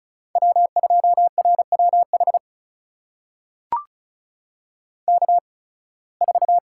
Morse code, heard as a keyed tone, about 34 wpm, spaced as for 23 wpm, 700 Hz. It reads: W2RWH K4
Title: USA Call Signs - 5 Characters - 8x Character Spacing - 35wpm